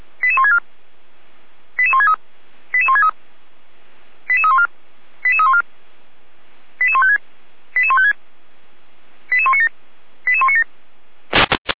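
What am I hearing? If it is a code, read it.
Five-tone selective call (ZVEI), heard as quick runs of five tones, tones of about 70 ms. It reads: 88164 88163 88163 88314 88314 88156 88156 88197 88197
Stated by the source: ZVEI five-tone call received by SDR, archived by Feuerwehr Orlamünde (fire brigade).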